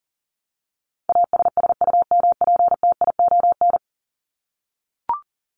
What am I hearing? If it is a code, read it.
Morse code, heard as a keyed tone, about 40 wpm, 700 Hz. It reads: ASSUMPTION
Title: Top 1751-2000 Words 40wpm